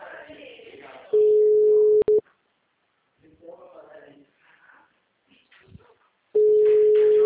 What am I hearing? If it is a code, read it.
Ringback tone (Europe, 425 Hz), with speech.